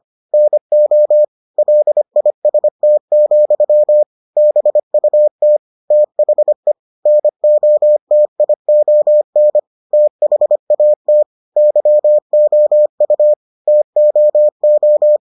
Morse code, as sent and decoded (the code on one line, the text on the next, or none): -. --- / .-.. .. ... - --..-- / -... ..- - / - .... / -. --- - .. --- -. / - .... .- - / -.-- --- ..- / - --- ---
NO LIST, BUT THE NOTION THAT YOU TOO